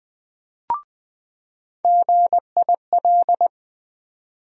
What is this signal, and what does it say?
Morse code, heard as a keyed tone, about 20 wpm, 700 Hz. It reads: GIL
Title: USA Call Sign Postfixes - 3 Letters 20wpm